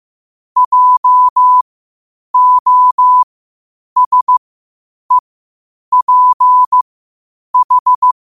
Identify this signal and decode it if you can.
Morse code, heard as a keyed tone, about 15 wpm, spaced as for 8 wpm, 1000 Hz. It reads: JOSEPH
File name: qotdfiles / 8wpm